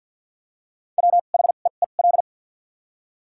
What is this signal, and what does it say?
Morse code, heard as a keyed tone, about 50 wpm, spaced as for 36 wpm, 700 Hz. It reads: WHEEL
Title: 2x Character Spacing - Top 500 Words - 50wpm